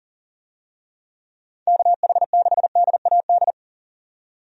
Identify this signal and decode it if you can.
Morse code, heard as a keyed tone, about 40 wpm, 700 Hz. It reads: KH6DAD
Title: DXCC-100 Call Signs - 6 Characters - 40wpm